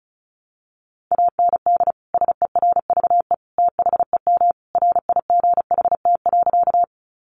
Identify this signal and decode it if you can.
Morse code, heard as a keyed tone, about 34 wpm, 700 Hz. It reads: AND SERVE THEM RIGHT.
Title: Sentences from Top 400 Words 35wpm